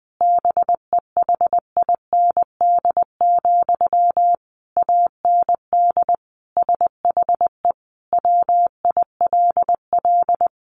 Morse code, heard as a keyed tone, about 20 wpm, 700 Hz. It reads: BEHIND, AND SHE WILL
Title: Sentences from Top 1500 Words 20wpm